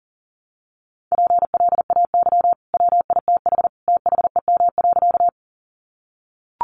Morse code, heard as a keyed tone, about 40 wpm, 700 Hz. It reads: PLAY WITH THEM.
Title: Sentences from Top 200 Words 40wpm